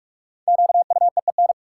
Morse code, heard as a keyed tone, about 45 wpm, 700 Hz. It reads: QUEEN
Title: Five Letter Words - 45wpm